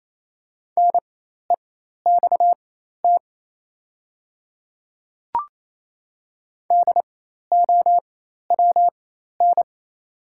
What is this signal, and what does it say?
Morse code, heard as a keyed tone, about 28 wpm, spaced as for 13 wpm, 700 Hz. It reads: NEXT DOWN
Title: 4x Character Spacing - Top 300 Words - 28wpm